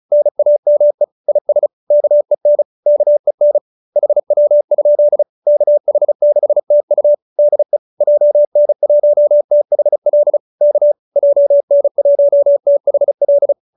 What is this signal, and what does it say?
Morse code, heard as a keyed tone, about 34 wpm, 590 Hz. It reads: NAME IS KEN KEN HW? KH6TU DE JN1THL K JN1THL